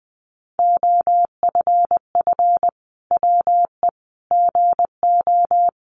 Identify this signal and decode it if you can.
Morse code, heard as a keyed tone, about 20 wpm, 700 Hz. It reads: OFF WE GO
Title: Sentences from Top 1400 Words 20wpm